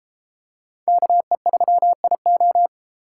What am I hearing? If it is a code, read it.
Morse code, heard as a keyed tone, about 33 wpm, 700 Hz. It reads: KE3IO